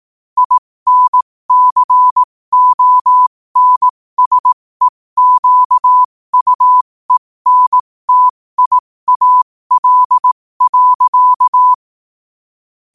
Morse code, heard as a keyed tone, about 18 wpm, spaced as for 15 wpm, 1000 Hz. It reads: INCONSEQUENTIAL.
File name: qotdfiles / 15wpm